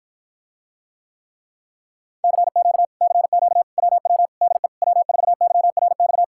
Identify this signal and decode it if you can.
Morse code, heard as a keyed tone, about 54 wpm, 700 Hz. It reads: CQ CQ WW DE W4YRX